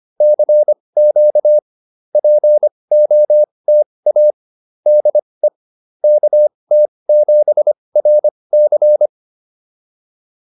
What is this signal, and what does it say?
Morse code, heard as a keyed tone, about 25 wpm, spaced as for 20 wpm, 590 Hz. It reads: CQ POTA DE KT7RC
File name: POTA207 20